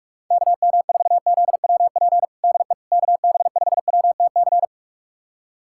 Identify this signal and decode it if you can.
Morse code, heard as a keyed tone, about 43 wpm, 700 Hz. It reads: KM4ZWP DE KB5WTC
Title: USA Callsign DE Callsign 45wpm